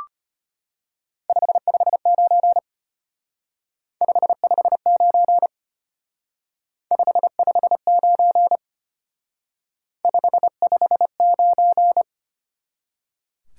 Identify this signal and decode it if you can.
Morse code, heard as a keyed tone, about 29 wpm, 700 Hz. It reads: 559 559 559 559
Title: Speed-Racing Top 90 QSO Elements 25wpm